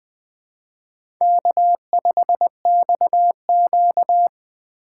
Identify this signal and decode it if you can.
Morse code, heard as a keyed tone, about 20 wpm, 700 Hz. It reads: K5XQ
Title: K1USN SST QSO Practice - 20wpm